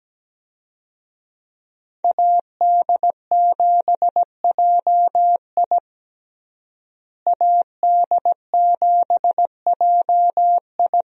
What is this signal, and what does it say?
Morse code, heard as a keyed tone, about 17 wpm, 700 Hz. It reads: AD7JI AD7JI
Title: USA Call Signs - 5 Characters - Repeated 2x - 17wpm